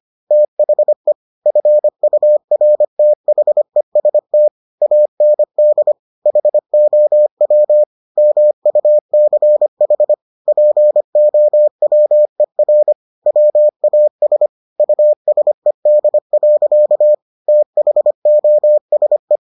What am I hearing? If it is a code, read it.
Morse code, heard as a keyed tone, about 25 wpm, 600 Hz. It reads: THE FURTHEST AND HOW MUCH POWER WAS USED. THOSE